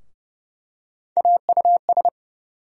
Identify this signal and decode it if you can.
Morse code, heard as a keyed tone, about 30 wpm, 700 Hz. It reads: AUS
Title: USA Call Sign Postfixes - 3 Letters 30wpm